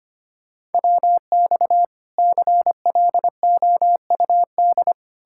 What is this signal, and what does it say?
Morse code, heard as a keyed tone, about 33 wpm, 700 Hz. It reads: WX CLOUD